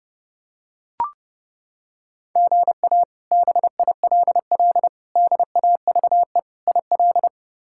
Morse code, heard as a keyed tone, about 32 wpm, 700 Hz. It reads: GA BILL DAVE IL
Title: K1USN SST QSO Practice - 30wpm